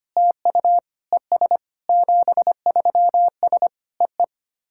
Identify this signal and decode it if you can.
Morse code, heard as a keyed tone, about 25 wpm, 700 Hz. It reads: TU ES 73S EE
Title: SOTA QSO Practice 25wpm